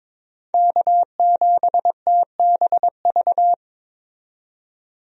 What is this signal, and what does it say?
Morse code, heard as a keyed tone, about 22 wpm, 700 Hz. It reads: K7TBV
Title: USA Call Signs - 5 Characters - 22wpm